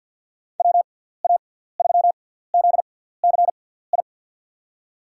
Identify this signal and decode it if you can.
Morse code, heard as a keyed tone, about 52 wpm, spaced as for 17 wpm, 700 Hz. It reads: WA3ZCI
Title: USA Call Signs - 6 Characters - 6x Character Spacing - 50wpm